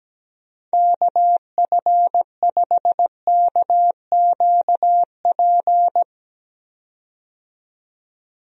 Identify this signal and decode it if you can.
Morse code, heard as a keyed tone, about 17 wpm, 700 Hz. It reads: KF5KQP
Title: USA Call Signs - 6 Characters - 17wpm